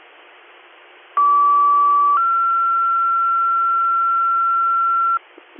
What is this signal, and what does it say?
Two-tone page, heard as a one-second tone, then a three-second tone, about 1150 then 1360 Hz, with noise. It is heard from 1.2 to 5.2 s.